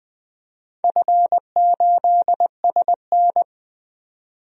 Morse code, heard as a keyed tone, about 20 wpm, 700 Hz. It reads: F8SN
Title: DXCC-100 Call Signs - 4 Characters - 20wpm